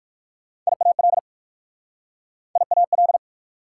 Morse code, heard as a keyed tone, about 60 wpm, 700 Hz. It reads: IAL IAL